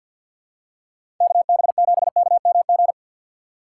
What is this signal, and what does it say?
Morse code, heard as a keyed tone, about 47 wpm, 700 Hz. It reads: KB7KMG
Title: USA Call Signs - 6 Characters - 4x Character Spacing - 50wpm